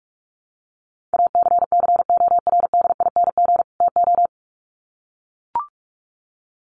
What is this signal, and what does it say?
Morse code, heard as a keyed tone, about 44 wpm, 700 Hz. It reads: ACCORDING TO
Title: Sets of 2 Words 45wpm